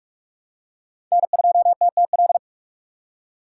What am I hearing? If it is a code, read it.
Morse code, heard as a keyed tone, about 46 wpm, 700 Hz. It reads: N2TTL
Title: USA Call Signs - 5 Characters - 2x Character Spacing - 45wpm